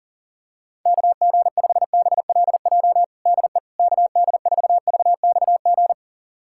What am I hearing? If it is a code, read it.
Morse code, heard as a keyed tone, about 37 wpm, 700 Hz. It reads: KG5BLJ DE KD4VXG
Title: USA Callsign DE Callsign 40wpm